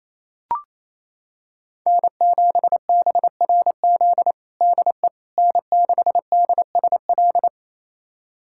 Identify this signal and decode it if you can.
Morse code, heard as a keyed tone, about 28 wpm, 700 Hz. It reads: N7BRZ DE N6DSL
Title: USA Callsign DE Callsign - 5 Characters - 28wpm